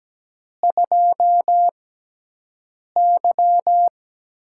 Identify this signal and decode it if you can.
Morse code, heard as a keyed tone, about 17 wpm, spaced as for 6 wpm, 700 Hz. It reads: 2Y
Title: USA Call Signs - 6 Characters - 6x Character Spacing - 17wpm